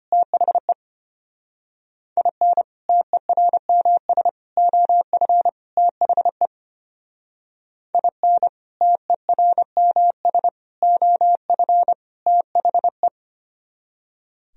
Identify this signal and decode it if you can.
Morse code, heard as a keyed tone, about 29 wpm, 700 Hz. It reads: THE IN TERMS OF THE IN TERMS OF THE
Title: Speed-Racing - Sets of 4 Words 25wpm